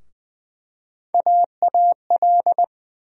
Morse code, heard as a keyed tone, about 20 wpm, 700 Hz. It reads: AAL
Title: USA Call Sign Postfixes - 3 Letters 20wpm